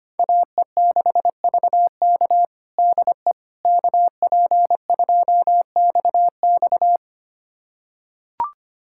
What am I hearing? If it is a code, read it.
Morse code, heard as a keyed tone, about 25 wpm, 700 Hz. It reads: AE6VK DE KP2XX
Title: USA Callsign DE Callsign - 5 Characters - 25wpm